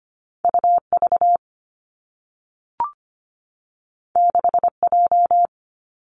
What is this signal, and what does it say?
Morse code, heard as a keyed tone, about 25 wpm, 700 Hz. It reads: UV 6J
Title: Two Challenging Letter-Number Combinations 25wpm